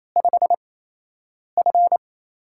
Morse code, heard as a keyed tone, about 28 wpm, spaced as for 8 wpm, 700 Hz. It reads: 5F
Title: USA Call Signs - 4 Characters - 8x Character Spacing - 28wpm